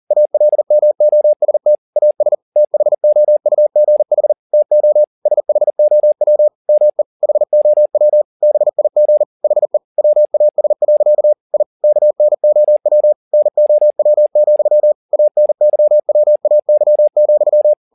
Morse code, heard as a keyed tone, about 40 wpm, 600 Hz. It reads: ALMOST AS THOUGH TO SHOW ME HOW BIG HE WAS. I KNOW NOW, ANYWAY,